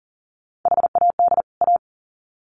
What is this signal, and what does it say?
Morse code, heard as a keyed tone, about 40 wpm, 700 Hz. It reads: HAD A
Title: Sets of 2 Words 40wpm